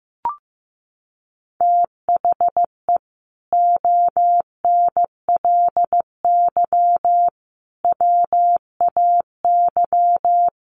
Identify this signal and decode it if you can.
Morse code, heard as a keyed tone, about 15 wpm, 700 Hz. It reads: THE ONLY WAY
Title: Sets of 4 Words 15wpm